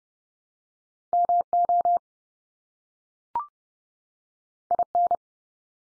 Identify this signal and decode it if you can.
Morse code, heard as a keyed tone, about 30 wpm, 700 Hz. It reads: MO IN
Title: US States 30WPM